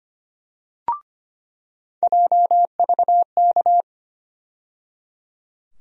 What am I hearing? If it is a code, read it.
Morse code, heard as a keyed tone, about 25 wpm, 700 Hz. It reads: JVK
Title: USA Call Sign Postfixes - 3 Letters 25wpm